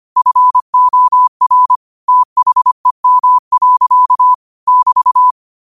Morse code, heard as a keyed tone, about 25 wpm, 1000 Hz. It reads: FOR THEM. <BT>